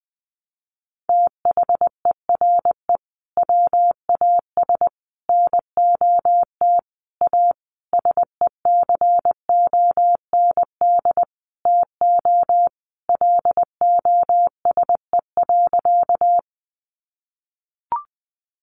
Morse code, heard as a keyed tone, about 20 wpm, 700 Hz. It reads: THERE WAS NOT A SECOND TO LOSE.
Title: Sentences from Top 1100 Words 20wpm